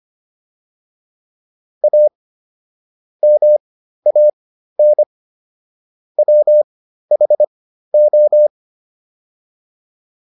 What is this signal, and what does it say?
Morse code, heard as a keyed tone, about 25 wpm, spaced as for 13 wpm, 600 Hz. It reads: A MAN WHO